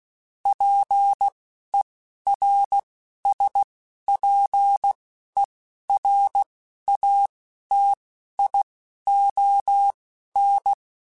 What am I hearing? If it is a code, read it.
Morse code, heard as a keyed tone, about 16 wpm, spaced as for 11 wpm, 800 Hz. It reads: PERSPERATION
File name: Quotes Spaced / Quotes Spaced-mp3s / Persperation sp @15